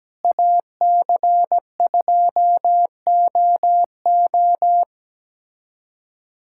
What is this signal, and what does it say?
Morse code, heard as a keyed tone, about 17 wpm, 700 Hz. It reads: AC2OO
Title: USA Callsign DE Callsign - 5 Characters - 17wpm